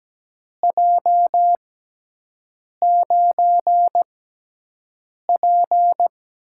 Morse code, heard as a keyed tone, about 17 wpm, spaced as for 6 wpm, 700 Hz. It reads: J9P
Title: USA Call Signs - 4 Characters - 6x Character Spacing - 17wpm